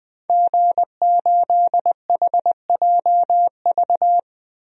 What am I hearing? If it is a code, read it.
Morse code, heard as a keyed tone, about 21 wpm, 700 Hz. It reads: G8HJV
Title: DXCC-100 Call Signs - 5 Characters - 20wpm